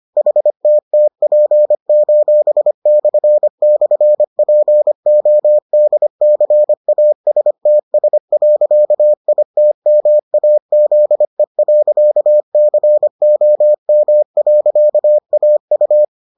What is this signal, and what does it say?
Morse code, heard as a keyed tone, about 25 wpm, 600 Hz. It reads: HTTP://PODCASTS.ITMAZE.COM.AU